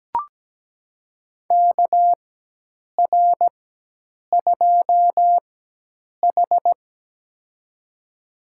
Morse code, heard as a keyed tone, about 17 wpm, spaced as for 8 wpm, 700 Hz. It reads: KR2H
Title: USA Call Signs - 4 Characters - 4x Character Spacing - 17wpm